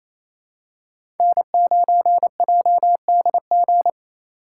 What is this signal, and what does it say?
Morse code, heard as a keyed tone, about 28 wpm, 700 Hz. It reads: N9JDG
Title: USA Call Signs - 5 Characters - 4x Character Spacing - 28wpm